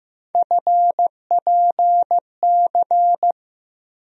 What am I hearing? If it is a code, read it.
Morse code, heard as a keyed tone, about 15 wpm, 700 Hz. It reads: FPC
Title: USA Call Sign Postfixes - 3 Letters 15wpm